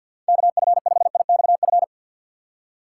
Morse code, heard as a keyed tone, about 52 wpm, 690 Hz. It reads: KF5IXF
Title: USA Call Signs - 6x Character Spacing - 50wpm